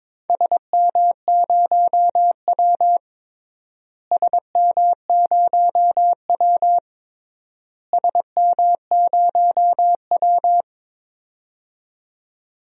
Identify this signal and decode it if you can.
Morse code, heard as a keyed tone, about 22 wpm, 700 Hz. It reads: SM0W SM0W SM0W
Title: DXCC-100 Call Signs - 4 Characters - Repeated 3x - 22wpm